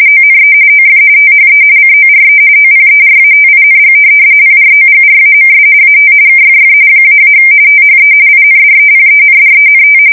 RTTY, about 45.45 baud. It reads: ONE OF THE CORE OPERATING PROGRAMS AT KFF -- THE INDEPENDENT
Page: 170 Shift 2125/2295